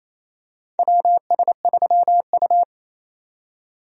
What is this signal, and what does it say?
Morse code, heard as a keyed tone, about 28 wpm, 700 Hz. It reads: WS3U
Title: USA Call Signs - 4 Characters - 28wpm